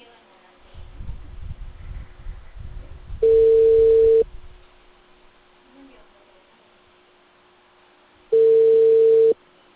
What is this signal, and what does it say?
Ringback tone (China, 450 Hz), with speech.